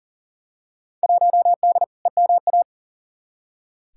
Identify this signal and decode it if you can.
Morse code, heard as a keyed tone, about 40 wpm, 700 Hz. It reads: 1D EMA
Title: Field Day Exchanges - 40wpm